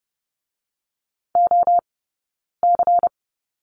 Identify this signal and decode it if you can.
Morse code, heard as a keyed tone, about 30 wpm, 700 Hz. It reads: O C
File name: 092 Lesson 16d - Letter c - Two Characters 30wpm